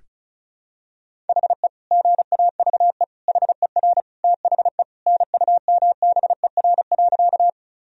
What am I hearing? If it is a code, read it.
Morse code, heard as a keyed tone, about 36 wpm, 700 Hz. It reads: HE GAVE HER THE NUMBER.
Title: Sentences from Top 1000 Words - Review Part 1 - 35wpm